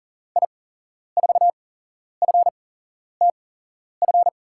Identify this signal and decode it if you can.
Morse code, heard as a keyed tone, about 40 wpm, spaced as for 11 wpm, 700 Hz. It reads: I4FTF
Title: USA Call Signs - 6 Characters - 8x Character Spacing - 40wpm